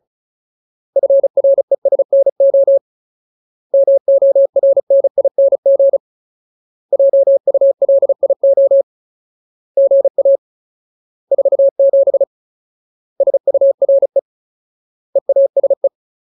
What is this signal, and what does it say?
Morse code, heard as a keyed tone, about 34 wpm, 550 Hz. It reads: FRESNO MORNING JULIO GA 47 SURE EASE
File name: Ultimate Word List / 35WPM Ultimate